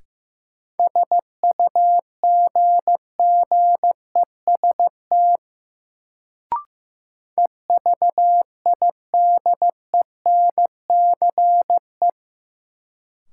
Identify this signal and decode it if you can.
Morse code, heard as a keyed tone, about 16 wpm, 700 Hz. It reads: SUGGEST EVIDENCE
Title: Sentences from Top 1300 Words - New Word Drill - 15wpm